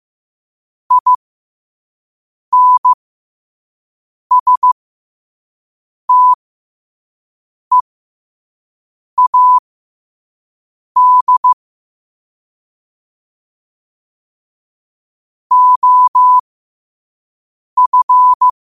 Morse code, heard as a keyed tone, about 15 wpm, spaced as for 5 wpm, 1000 Hz. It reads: INSTEAD OF
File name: qotdfiles / 5wpm